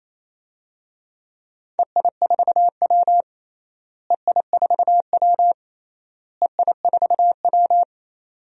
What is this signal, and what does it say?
Morse code, heard as a keyed tone, about 28 wpm, 700 Hz. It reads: EI4W EI4W EI4W
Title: DXCC-100 Call Signs - 4 Characters - Repeated 3x - 28wpm